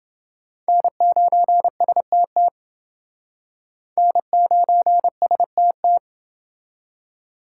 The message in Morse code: N9STT N9STT

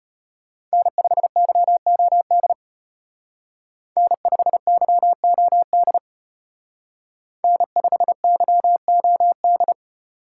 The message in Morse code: N5YOD N5YOD N5YOD

